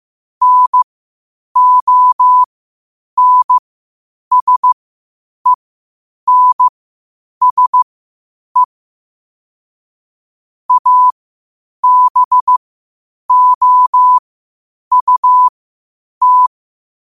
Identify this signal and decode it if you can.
Morse code, heard as a keyed tone, about 15 wpm, spaced as for 8 wpm, 1000 Hz. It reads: NONSENSE ABOUT
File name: qotdfiles / 8wpm